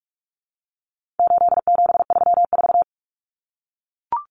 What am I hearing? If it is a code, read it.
Morse code, heard as a keyed tone, about 44 wpm, 700 Hz. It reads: 8734
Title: Common Cut Numbers - 4-Digit 45wpm